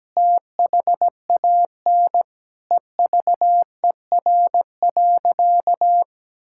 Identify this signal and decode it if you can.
Morse code, heard as a keyed tone, about 17 wpm, 700 Hz. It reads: THAN EVER.
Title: Sentences from Top 1000 Words - Review Part 6 - 17wpm